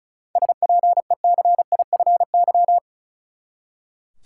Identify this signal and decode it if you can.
Morse code, heard as a keyed tone, about 34 wpm, 700 Hz. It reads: SPECIFY
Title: Seven Letter Words - 35wpm